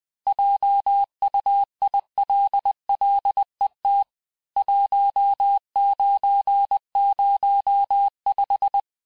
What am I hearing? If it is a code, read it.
Morse code, heard as a keyed tone, about 20 wpm, 800 Hz. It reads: JUILLET 1905
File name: cours-du-lundi files / cours-1584-teste-clair-vitesse-19mpm-puis-24mpm